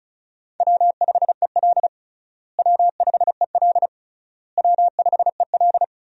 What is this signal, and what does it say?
Morse code, heard as a keyed tone, about 36 wpm, 700 Hz. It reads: W5EL W5EL W5EL
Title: USA Call Signs - 4 Characters - Repeated 3x - 35wpm